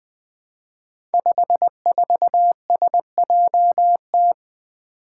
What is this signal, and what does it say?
Morse code, heard as a keyed tone, about 20 wpm, 700 Hz. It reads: 54SJT
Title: Five Challenging Letter-Number Combinations 20wpm